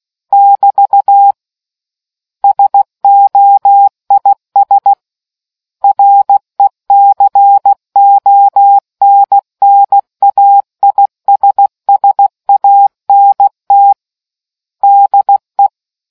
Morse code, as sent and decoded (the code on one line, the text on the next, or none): -...- / ... --- .. ... / .-. -.-. --- -. -. .- .. ... ... .- -. - / -.. .
<BT> SOIS RECONNAISSANT DE